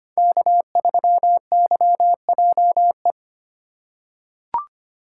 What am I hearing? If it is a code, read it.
Morse code, heard as a keyed tone, about 25 wpm, 700 Hz. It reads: K3YJE
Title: USA Call Signs - 4x Character Spacing - 25wpm